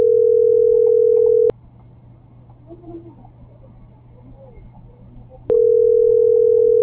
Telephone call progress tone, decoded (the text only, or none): ringback tone